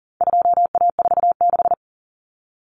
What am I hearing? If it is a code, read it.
Morse code, heard as a keyed tone, about 40 wpm, 700 Hz. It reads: 2A46
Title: Common Cut Numbers - 4-Digit 40wpm